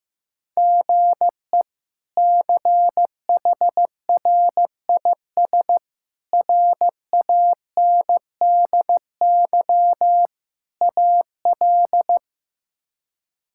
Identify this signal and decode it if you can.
Morse code, heard as a keyed tone, about 15 wpm, 700 Hz. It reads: GE CHRIS RANDY AL
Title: K1USN SST QSO Practice - 15wpm